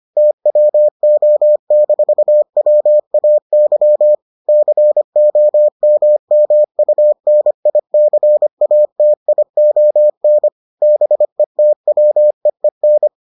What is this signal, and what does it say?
Morse code, heard as a keyed tone, about 25 wpm, 600 Hz. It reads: TWO-WAY COMMUNICATION BETWEEN